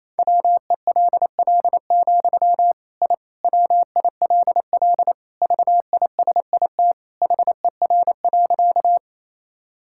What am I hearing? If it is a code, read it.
Morse code, heard as a keyed tone, about 28 wpm, 700 Hz. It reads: WELL, I WILL VISIT HER.
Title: Sentences from Top 1000 Words - Review Part 6 - 28wpm